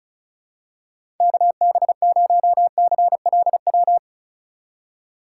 Morse code, heard as a keyed tone, about 34 wpm, 700 Hz. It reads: KB0CLW
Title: USA Call Signs - 6 Characters - 4x Character Spacing - 35wpm